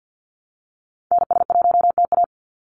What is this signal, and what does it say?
Morse code, heard as a keyed tone, about 50 wpm, 700 Hz. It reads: NS1TA
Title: NA SOTA Activator Callsigns 50wpm